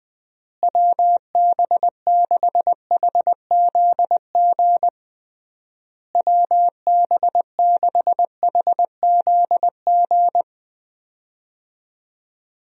Morse code, heard as a keyed tone, about 20 wpm, 700 Hz. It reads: WB6HZG WB6HZG
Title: USA Call Signs - 6 Characters - Repeated 3x - 20wpm